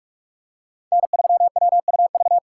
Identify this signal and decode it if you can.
Morse code, heard as a keyed tone, about 47 wpm, 700 Hz. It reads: N3WUV
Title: USA Call Signs - 5 Characters - 6x Character Spacing - 45wpm